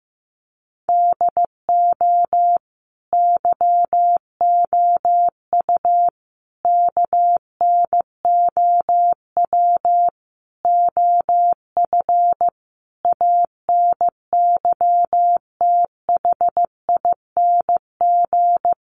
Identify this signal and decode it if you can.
Morse code, heard as a keyed tone, about 15 wpm, 700 Hz. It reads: DO YOU KNOW OF ANYTHING
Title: Sentences from Top 1200 Words 15wpm